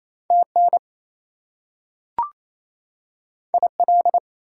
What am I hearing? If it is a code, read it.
Morse code, heard as a keyed tone, about 28 wpm, 700 Hz. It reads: TN IL
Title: US States and Canadian Provinces 28wpm